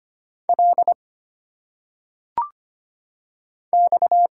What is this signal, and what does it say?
Morse code, heard as a keyed tone, about 25 wpm, 700 Hz. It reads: L X
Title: Single Letters - Phonetic Pronunciation 25wpm